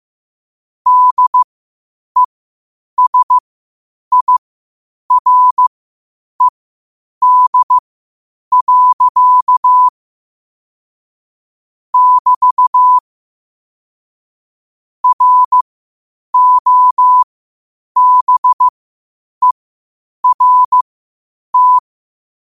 Morse code, heard as a keyed tone, about 15 wpm, spaced as for 8 wpm, 1000 Hz. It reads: DESIRED. <BT> ROBERT
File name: qotdfiles / 8wpm